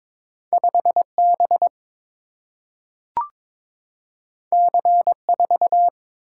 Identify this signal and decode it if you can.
Morse code, heard as a keyed tone, about 22 wpm, 700 Hz. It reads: 5 B C 4
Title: Single-Letter Single-Number 22wpm